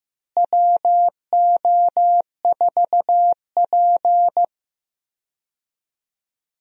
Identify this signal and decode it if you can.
Morse code, heard as a keyed tone, about 16 wpm, 700 Hz. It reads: WO4P
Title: USA Call Signs - 4 Characters - Repeated 3x - 15wpm